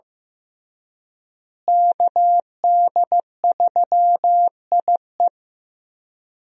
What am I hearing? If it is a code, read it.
Morse code, heard as a keyed tone, about 15 wpm, 700 Hz. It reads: KD3IE
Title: USA Call Signs - 5 Characters - Repeated 3x - 15wpm